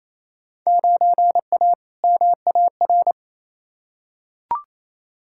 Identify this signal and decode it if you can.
Morse code, heard as a keyed tone, about 28 wpm, 700 Hz. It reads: 9A MAR
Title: Field Day Exchanges Encore - 28wpm